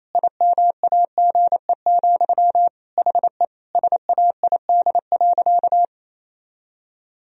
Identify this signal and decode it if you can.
Morse code, heard as a keyed tone, about 28 wpm, 700 Hz. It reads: IMAGE, HE SAID.
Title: Sentences from Top 1000 Words - Review Part 4 - 28wpm